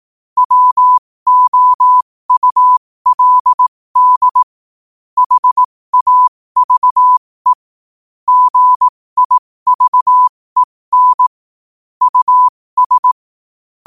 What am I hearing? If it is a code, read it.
Morse code, heard as a keyed tone, about 18 wpm, spaced as for 15 wpm, 1000 Hz. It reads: WOULD HAVE GIVEN US
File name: qotdfiles / 15wpm